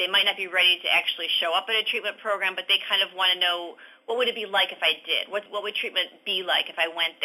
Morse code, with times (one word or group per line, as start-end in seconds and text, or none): none